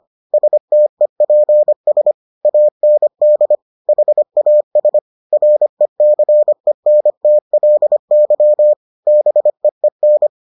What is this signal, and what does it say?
Morse code, heard as a keyed tone, about 25 wpm, 600 Hz. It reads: STEPS AND HAS RECENTLY BEEN